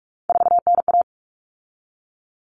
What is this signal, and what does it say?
Morse code, heard as a keyed tone, about 44 wpm, 700 Hz. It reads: 4NA
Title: Common Cut Numbers - 3-Digit 45wpm